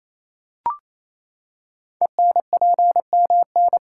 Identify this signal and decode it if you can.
Morse code, heard as a keyed tone, about 27 wpm, 700 Hz. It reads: ENPMN
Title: Code Groups - 5 Letter:Numbers - 28wpm